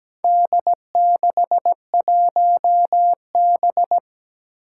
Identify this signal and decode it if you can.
Morse code, heard as a keyed tone, about 17 wpm, 700 Hz. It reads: D61B
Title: Four Challenging Letter-Number Combinations 17wpm